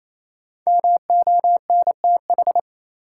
Morse code, heard as a keyed tone, about 28 wpm, 700 Hz. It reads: MONTH
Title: Sentences from Top 800 Words - New Word Drill - 28wpm